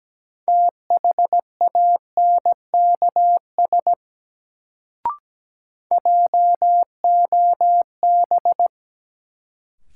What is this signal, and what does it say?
Morse code, heard as a keyed tone, about 17 wpm, 700 Hz. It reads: THANKS JOB
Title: Sentences from Top 600 Words - New Word Drill - 17wpm